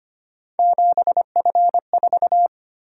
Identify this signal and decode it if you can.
Morse code, heard as a keyed tone, about 25 wpm, 700 Hz. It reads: 7F4
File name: Three Letter-Number Combinations - 25wpm